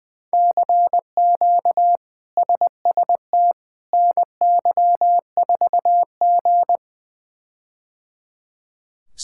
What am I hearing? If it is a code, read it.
Morse code, heard as a keyed tone, about 20 wpm, 700 Hz. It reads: CQ SST NY4G
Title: K1USN SST QSO Practice - 20wpm